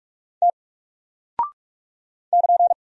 Morse code, heard as a keyed tone, about 44 wpm, 700 Hz. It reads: T <KN>
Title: Single Character (Letter:Number:Punctuation:Prosign) Alt 45wpm